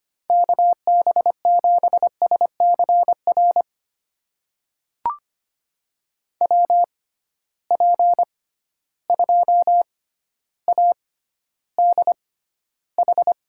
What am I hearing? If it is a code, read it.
Morse code, heard as a keyed tone, about 25 wpm, spaced as for 13 wpm, 700 Hz. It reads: KB7SCR WP2ADH
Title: USA Call Signs - 6 Characters - 6x Character Spacing - 25wpm